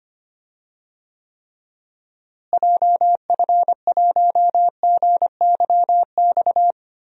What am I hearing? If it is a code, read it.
Morse code, heard as a keyed tone, about 25 wpm, 700 Hz. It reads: JF1GYX
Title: DXCC-100 Call Signs - 6 Characters - Repeated 3x - 25wpm